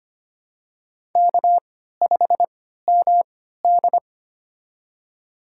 Morse code, heard as a keyed tone, about 25 wpm, spaced as for 14 wpm, 700 Hz. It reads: K5MD
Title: USA Call Signs - 4 Characters - 3x Character Spacing - 25wpm